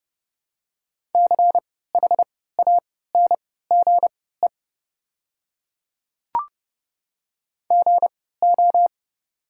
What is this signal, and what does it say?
Morse code, heard as a keyed tone, about 30 wpm, spaced as for 17 wpm, 700 Hz. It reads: CHANGE GO
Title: 3x Character Spacing - Top 500 Words - 30wpm